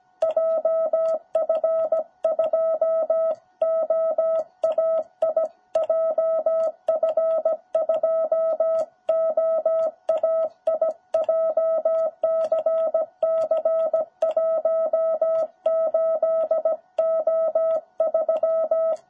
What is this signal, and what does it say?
Morse code, heard as a keyed tone, about 17 wpm, 660 Hz. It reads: JF2OAI JF2OAI JCC18O3
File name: jh8cbh / 2021cw